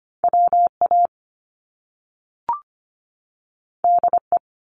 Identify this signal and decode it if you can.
Morse code, heard as a keyed tone, about 25 wpm, 700 Hz. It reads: WA DE
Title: US States 25wpm